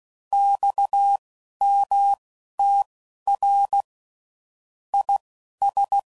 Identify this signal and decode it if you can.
Morse code, heard as a keyed tone, about 16 wpm, spaced as for 11 wpm, 800 Hz. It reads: XMTR IS
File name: CW Code Groups @10WPM / QSOs / QSO WA7QOX